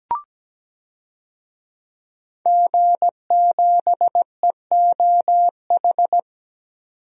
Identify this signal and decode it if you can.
Morse code, heard as a keyed tone, about 17 wpm, 700 Hz. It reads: G7EOH
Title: DXCC-100 Call Signs - 5 Characters - 17wpm